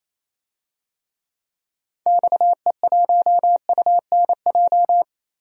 Morse code, heard as a keyed tone, about 28 wpm, 700 Hz. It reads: XE1UNJ
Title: DXCC-100 Call Signs - 6 Characters - 28wpm